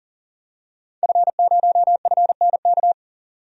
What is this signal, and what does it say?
Morse code, heard as a keyed tone, about 40 wpm, 700 Hz. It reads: F0FNK
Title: DXCC-100 Call Signs - 5 Characters - 40wpm